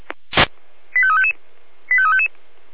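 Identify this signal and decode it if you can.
Five-tone selective call (ZVEI), heard as quick runs of five tones, tones of about 70 ms. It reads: 85355 85355
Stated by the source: ZVEI five-tone selective call received via SDR.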